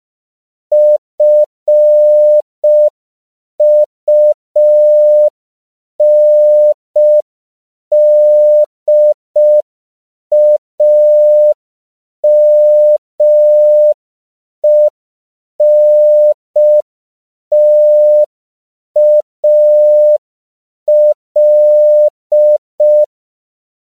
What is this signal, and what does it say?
Morse code, heard as a keyed tone, about 5 wpm, 600 Hz. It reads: FUNDAMENTAL
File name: CW / 5PPM